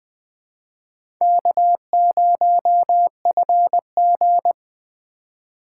Morse code, heard as a keyed tone, about 20 wpm, 700 Hz. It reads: K0FG
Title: USA Call Signs - 4 Characters - 3x Character Spacing - 20wpm